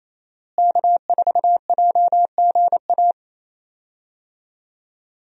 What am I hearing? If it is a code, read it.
Morse code, heard as a keyed tone, about 27 wpm, 700 Hz. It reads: K4JGA
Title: USA Call Signs - 5 Characters - Repeated 3x - 28wpm